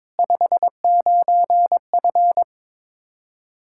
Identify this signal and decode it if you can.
Morse code, heard as a keyed tone, about 22 wpm, 700 Hz. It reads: 59F
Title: K1USN SST QSO Practice 22wpm